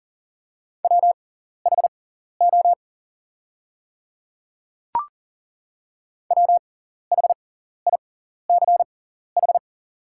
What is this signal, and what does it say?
Morse code, heard as a keyed tone, about 40 wpm, spaced as for 14 wpm, 700 Hz. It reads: WHO WHICH
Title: 6x Character Spacing - Top 100 Words - 40wpm